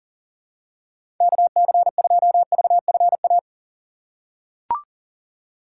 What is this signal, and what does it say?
Morse code, heard as a keyed tone, about 39 wpm, 700 Hz. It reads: KC2VFA
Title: USA Call Signs - 6 Characters - 4x Character Spacing - 40wpm